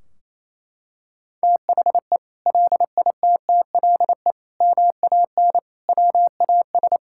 Morse code, heard as a keyed tone, about 28 wpm, 700 Hz. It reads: THE LITTLE MAN WAS
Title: Sentences from Top 1600 Words 28wpm